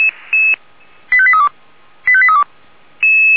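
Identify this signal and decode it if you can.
Five-tone selective call (ZVEI), heard as quick runs of five tones, tones of about 70 ms. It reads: 85832 85832
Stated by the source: ZVEI five-tone selective call received on an SDR.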